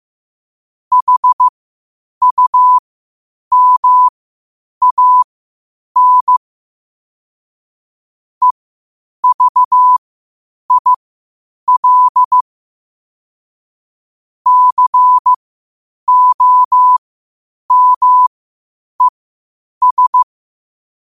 Morse code, heard as a keyed tone, about 15 wpm, spaced as for 8 wpm, 1000 Hz. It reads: HUMAN EVIL COMES
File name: qotdfiles / 8wpm